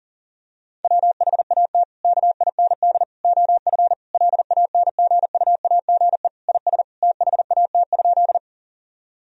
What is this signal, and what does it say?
Morse code, heard as a keyed tone, about 40 wpm, 700 Hz. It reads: WHAT KIND OF LANGUAGE IS THAT?
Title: Sentences from Top 1000 Words - Review Part 3 - 40wpm